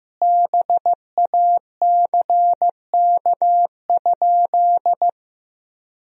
Morse code, heard as a keyed tone, about 16 wpm, 700 Hz. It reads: BACK?
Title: 4x Word-Spacing - Sentences from Top 200 Words - 15wpm